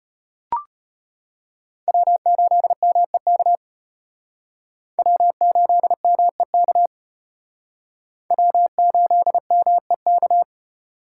Morse code, heard as a keyed tone, about 33 wpm, 700 Hz. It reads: W8MEK W8MEK W8MEK